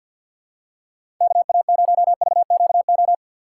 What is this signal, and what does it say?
Morse code, heard as a keyed tone, about 47 wpm, 700 Hz. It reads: KA0VQO
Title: USA Call Signs - 6 Characters - 3x Character Spacing - 50wpm